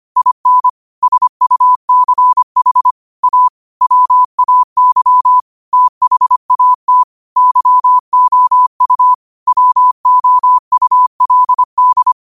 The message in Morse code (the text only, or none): IN SUCH A WAY THAT YOU WOULD